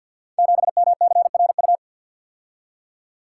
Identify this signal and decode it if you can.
Morse code, heard as a keyed tone, about 52 wpm, 700 Hz. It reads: 7MCRU
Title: Code Groups - 5 Letter:Numbers - 50wpm